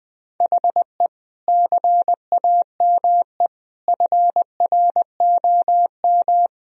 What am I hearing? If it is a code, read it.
Morse code, heard as a keyed tone, about 20 wpm, 700 Hz. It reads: HE CAME FROM
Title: Sentences from Top 1000 Words - Review Part 2 - 20wpm